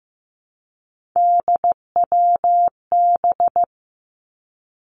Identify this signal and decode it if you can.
Morse code, heard as a keyed tone, about 15 wpm, 700 Hz. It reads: DWB
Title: Three Challenging Letter-Number Combinations 15wpm